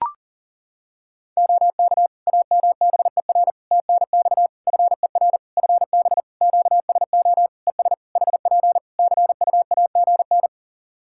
Morse code, heard as a keyed tone, about 40 wpm, 700 Hz. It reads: OK AMBER TNX FER FB QSO ES HP CUAGN
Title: Practice QSO - Exchange 4 - Ending - 40wpm